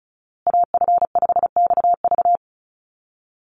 Morse code, heard as a keyed tone, about 34 wpm, 700 Hz. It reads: AF5XV
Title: USA Call Signs Repeated 2x 35wpm